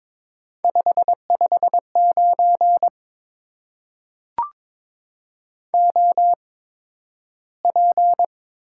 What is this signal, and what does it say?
Morse code, heard as a keyed tone, about 22 wpm, spaced as for 12 wpm, 700 Hz. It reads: 559 OP
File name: Top 90 QSO Elements - 8x Character Spacing - 22wpm